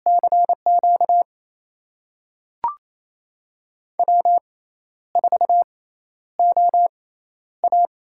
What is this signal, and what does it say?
Morse code, heard as a keyed tone, about 28 wpm, spaced as for 12 wpm, 700 Hz. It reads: CQ W4OA
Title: USA Call Signs - 6x Character Spacing - 28wpm